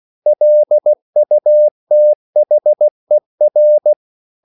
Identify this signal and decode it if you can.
Morse code, heard as a keyed tone, about 16 wpm, 600 Hz. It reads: LUTHER
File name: CW / Practice / EnglishQuotations0116